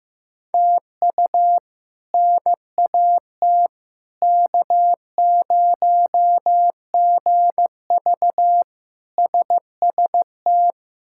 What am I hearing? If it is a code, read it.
Morse code, heard as a keyed tone, about 16 wpm, 700 Hz. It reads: TU NAT K0GV SST
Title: K1USN SST QSO Practice - 15wpm